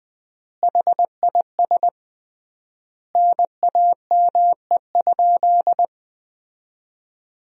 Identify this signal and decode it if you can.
Morse code, heard as a keyed tone, about 20 wpm, 700 Hz. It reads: HIS NAME?
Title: 3x Word-Spacing - Sentences from Top 400 Words - 20wpm